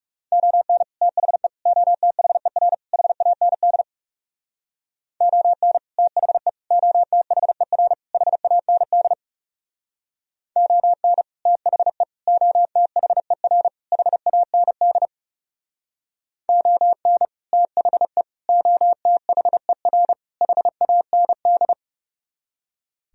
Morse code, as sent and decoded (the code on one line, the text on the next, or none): --- -. / - .... / --- - .... .-. / .... .- -. -.. / --- -. / - .... / --- - .... .-. / .... .- -. -.. / --- -. / - .... / --- - .... .-. / .... .- -. -.. / --- -. / - .... / --- - .... .-. / .... .- -. -..
ON THE OTHER HAND ON THE OTHER HAND ON THE OTHER HAND ON THE OTHER HAND